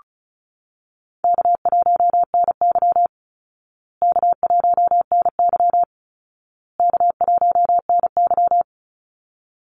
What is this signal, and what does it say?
Morse code, heard as a keyed tone, about 34 wpm, 700 Hz. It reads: K1NY K1NY K1NY
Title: Easy Call Signs Repeated 3x - 35wpm